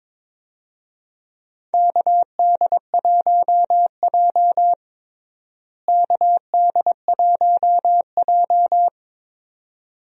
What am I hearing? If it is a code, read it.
Morse code, heard as a keyed tone, about 22 wpm, 700 Hz. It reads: KD1J KD1J
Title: USA Call Signs - 4 Characters - Repeated 2x - 22wpm